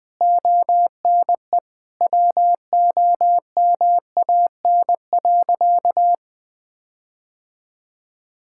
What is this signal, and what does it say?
Morse code, heard as a keyed tone, about 21 wpm, 700 Hz. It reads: ONE WOMAN.
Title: Sentences with 6-Word Phrases - 20wpm